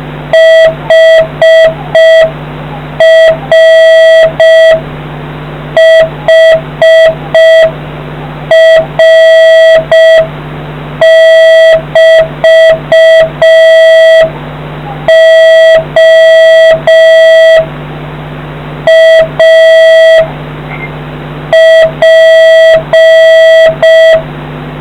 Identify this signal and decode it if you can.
Morse code, heard as a keyed tone, about 6 wpm, 650 Hz, with noise. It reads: HRHR<BT>O A P